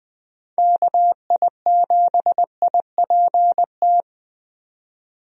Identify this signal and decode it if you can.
Morse code, heard as a keyed tone, about 21 wpm, 700 Hz. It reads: KI7IPT
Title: USA Call Signs - 6 Characters - 20wpm